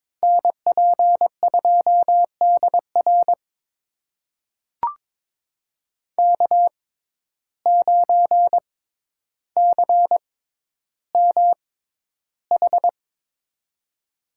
Morse code, heard as a keyed tone, about 22 wpm, spaced as for 11 wpm, 700 Hz. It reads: NP2DR K9CMH
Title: USA Call Signs - 5 Characters - 6x Character Spacing - 22wpm